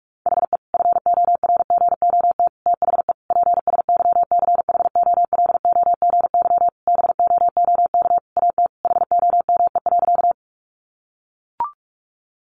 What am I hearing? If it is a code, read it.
Morse code, heard as a keyed tone, about 44 wpm, 700 Hz. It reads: HE FORGOT THE PSYCHOLOGY BOOK AT HOME.
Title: Sentences from Top 1400 Words 45wpm